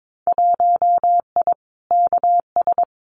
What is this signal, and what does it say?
Morse code, heard as a keyed tone, about 22 wpm, 700 Hz. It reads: 1I KS
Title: Winter Field Day Exchanges Encore 22wpm